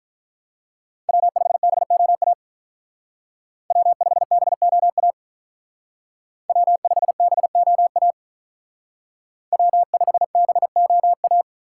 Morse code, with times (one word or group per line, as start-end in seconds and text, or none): 1.1-2.3 W5BOA
3.7-5.1 W5BOA
6.5-8.1 W5BOA
9.5-11.4 W5BOA